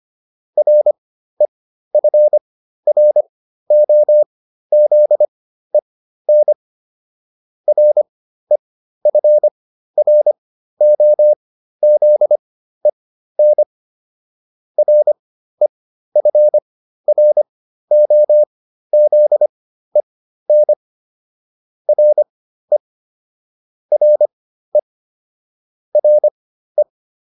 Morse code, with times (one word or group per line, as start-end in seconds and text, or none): 0.6-6.5 REFROZEN
7.7-13.6 REFROZEN
14.8-20.7 REFROZEN
21.9-22.8 RE
23.9-24.8 RE
25.9-26.8 RE